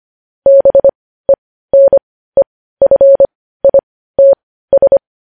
Morse code, heard as a keyed tone, about 25 wpm, spaced as for 15 wpm, 550 Hz. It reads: BENEFITS
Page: news-headlines-in-morse-code-at-15-wpm